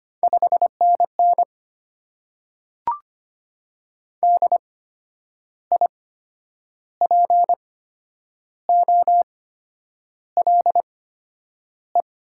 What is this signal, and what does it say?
Morse code, heard as a keyed tone, about 25 wpm, spaced as for 9 wpm, 700 Hz. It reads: 5NN DIPOLE
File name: Top 90 QSO Elements - 8x Character Spacing - 25wpm